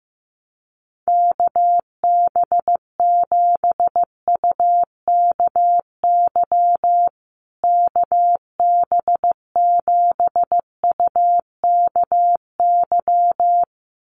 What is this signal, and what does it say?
Morse code, heard as a keyed tone, about 15 wpm, 700 Hz. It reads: KB7UKY KB7UKY
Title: USA Call Signs Repeated 2x 15wpm